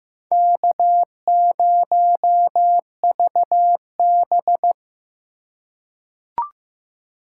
Morse code, heard as a keyed tone, about 15 wpm, 700 Hz. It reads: K0VB
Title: USA Call Signs - 4x Character Spacing - 15wpm